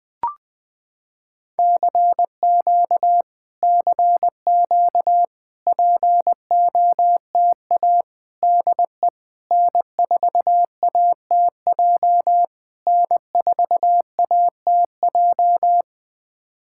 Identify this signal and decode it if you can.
Morse code, heard as a keyed tone, about 20 wpm, 700 Hz. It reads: CQ CQ POTA DE N4ATJ N4ATJ
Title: POTA QSO Practice - 20wpm